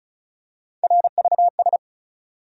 Morse code, heard as a keyed tone, about 34 wpm, 700 Hz. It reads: RVS
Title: USA Call Sign Postfixes - 3 Letters 35wpm